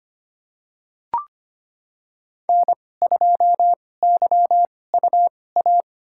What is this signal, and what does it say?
Morse code, heard as a keyed tone, about 25 wpm, spaced as for 18 wpm, 700 Hz. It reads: N2YUA
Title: USA Call Signs - 2x Character Spacing - 25wpm